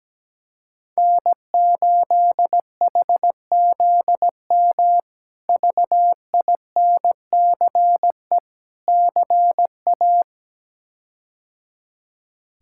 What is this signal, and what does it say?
Morse code, heard as a keyed tone, about 17 wpm, 700 Hz. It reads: N8HZM VINCE CA
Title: K1USN SST QSO Practice - 17wpm